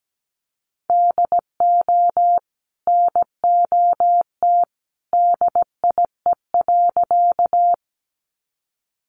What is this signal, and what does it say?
Morse code, heard as a keyed tone, about 17 wpm, 700 Hz. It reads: DO NOT DIE.
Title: Sentences from Top 1300 Words 17wpm